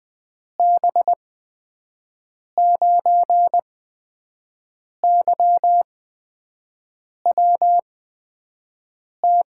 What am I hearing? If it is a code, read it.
Morse code, heard as a keyed tone, about 20 wpm, spaced as for 5 wpm, 700 Hz. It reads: B9YWT
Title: USA Call Signs - 6 Characters - 8x Character Spacing - 20wpm